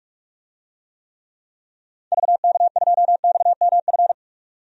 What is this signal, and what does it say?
Morse code, heard as a keyed tone, about 47 wpm, 700 Hz. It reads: VK2XMF
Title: DXCC-100 Call Signs - 6 Characters - 45wpm